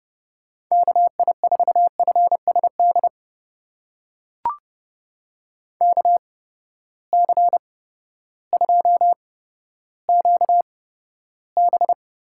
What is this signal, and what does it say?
Morse code, heard as a keyed tone, about 30 wpm, spaced as for 14 wpm, 700 Hz. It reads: KI4FSD KC2QB
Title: USA Call Signs - 6 Characters - 8x Character Spacing - 30wpm